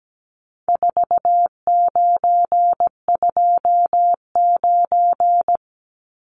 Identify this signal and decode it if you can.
Morse code, heard as a keyed tone, about 17 wpm, 700 Hz. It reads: 4929
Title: Numbers - 4-Digit 17wpm